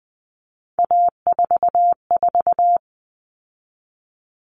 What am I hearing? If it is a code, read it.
Morse code, heard as a keyed tone, about 20 wpm, 700 Hz. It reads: A44
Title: Common Cut Numbers - 3-Digit 20wpm